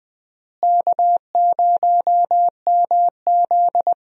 Morse code, heard as a keyed tone, about 20 wpm, 700 Hz. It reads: K0MZ
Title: USA Call Signs - 4 Characters - Repeated 3x - 20wpm